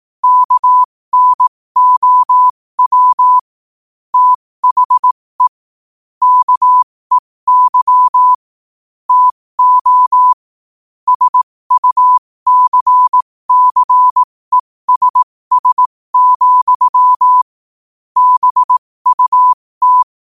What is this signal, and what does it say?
Morse code, heard as a keyed tone, about 18 wpm, spaced as for 15 wpm, 1000 Hz. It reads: KNOW THE KEY TO SUCCESS, BUT
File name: qotdfiles / 15wpm